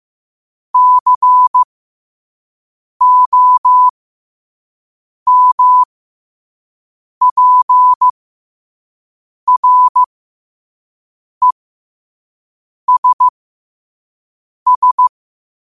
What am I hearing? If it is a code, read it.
Morse code, heard as a keyed tone, about 15 wpm, spaced as for 5 wpm, 1000 Hz. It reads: COMPRESS
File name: qotdfiles / 5wpm